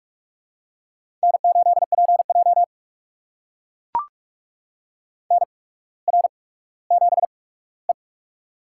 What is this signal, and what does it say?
Morse code, heard as a keyed tone, about 41 wpm, spaced as for 19 wpm, 700 Hz. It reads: N8PJ NR7E
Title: USA Call Signs - 4 Characters - 8x Character Spacing - 45wpm